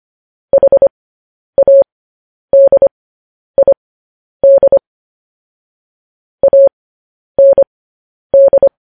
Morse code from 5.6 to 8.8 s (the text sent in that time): AND